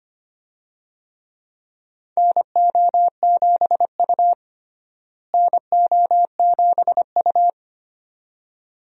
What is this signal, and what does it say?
Morse code, heard as a keyed tone, about 25 wpm, 700 Hz. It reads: NO7U NO7U
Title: USA Call Signs - 4 Characters - Repeated 2x - 25wpm